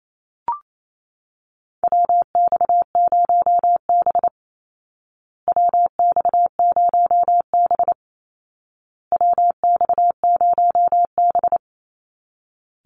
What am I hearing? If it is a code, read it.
Morse code, heard as a keyed tone, about 28 wpm, 700 Hz. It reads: WX0B WX0B WX0B
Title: Call Signs Repeated 2x - 28wpm